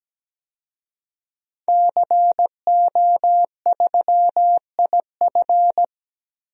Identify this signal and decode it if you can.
Morse code, heard as a keyed tone, about 17 wpm, 700 Hz. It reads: CO3IF
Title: DXCC-100 Call Signs - 5 Characters - 17wpm